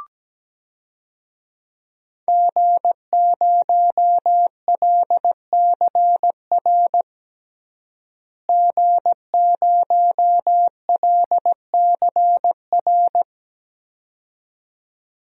Morse code, heard as a keyed tone, about 17 wpm, 700 Hz. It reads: G0LCR G0LCR
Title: DXCC-100 Call Signs - 5 Characters - Repeated 2x - 17wpm